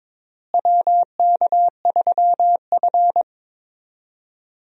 Morse code, heard as a keyed tone, about 22 wpm, 700 Hz. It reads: WK3F
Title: USA Call Signs - 4 Characters - Repeated 2x - 22wpm